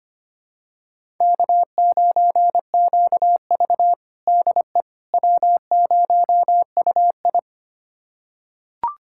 Morse code, heard as a keyed tone, about 25 wpm, 700 Hz. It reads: K9QV DE W0UI
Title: USA Callsign DE Callsign - 4 Characters - 25wpm